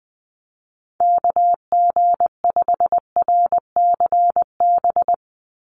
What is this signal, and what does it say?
Morse code, heard as a keyed tone, about 20 wpm, 700 Hz. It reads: KG5RCB
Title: USA Call Signs 20wpm